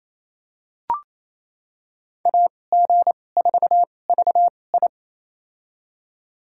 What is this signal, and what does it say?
Morse code, heard as a keyed tone, about 28 wpm, spaced as for 20 wpm, 700 Hz. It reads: AG4VI
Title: USA Call Signs - 5 Characters - 2x Character Spacing - 28wpm